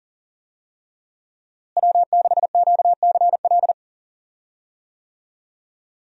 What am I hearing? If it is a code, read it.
Morse code, heard as a keyed tone, about 40 wpm, 700 Hz. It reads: W6QCL
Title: USA Call Signs - 5 Characters - Repeated 3x - 40wpm